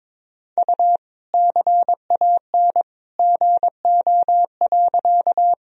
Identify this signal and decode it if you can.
Morse code, heard as a keyed tone, about 22 wpm, 700 Hz. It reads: U CAN GO.
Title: Sentences from Top 100 Words and Abbreviations 22wpm